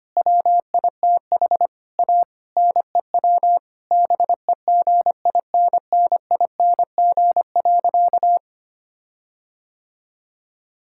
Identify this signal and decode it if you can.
Morse code, heard as a keyed tone, about 25 wpm, 700 Hz. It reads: WITH A NEW BEGINNING.
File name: practice / 25wpm / Difficult Sentences with Hints - 25wpm